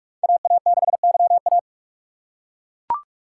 Morse code, heard as a keyed tone, about 41 wpm, 700 Hz. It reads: AA6YA